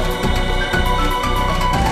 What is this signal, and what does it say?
There is no code.